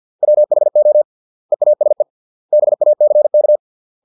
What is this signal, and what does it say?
Morse code, heard as a keyed tone, about 50 wpm, 600 Hz. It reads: WHO EASE BACK